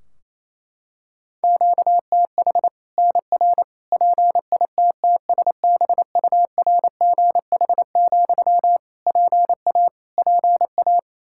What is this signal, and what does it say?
Morse code, heard as a keyed tone, about 27 wpm, 700 Hz. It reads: QTH NR PITTSBURGH, PA PA
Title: Practice QSO - Exchange 1 - Beginning - 28wpm